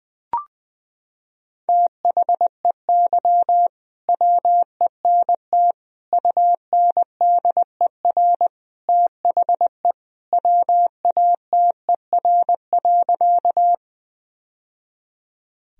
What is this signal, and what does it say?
Morse code, heard as a keyed tone, about 21 wpm, 700 Hz. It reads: THEY WENT UNDER THE WATER.
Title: Sentences from Top 1000 Words - Review Part 1 - 20wpm